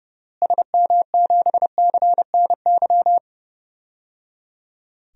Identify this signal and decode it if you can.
Morse code, heard as a keyed tone, about 30 wpm, 700 Hz. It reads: SM7CNY
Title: DXCC-100 Call Signs - 6 Characters - Repeated 3x - 30wpm